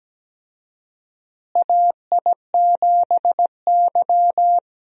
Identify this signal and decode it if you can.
Morse code, heard as a keyed tone, about 17 wpm, 700 Hz. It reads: AI7Y
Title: USA Call Signs - 4 Characters - Repeated 3x - 17wpm